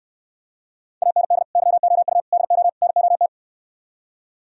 Morse code, heard as a keyed tone, about 67 wpm, 700 Hz. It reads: AND YOU DO NOT